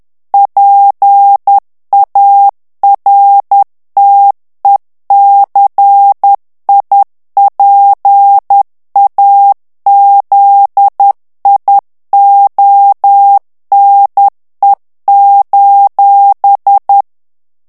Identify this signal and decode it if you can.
Morse code, heard as a keyed tone, about 10 wpm, 800 Hz. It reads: PARTECIPAZIONE:
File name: ITALIEN texte clair 2 vitesse 10 mots minute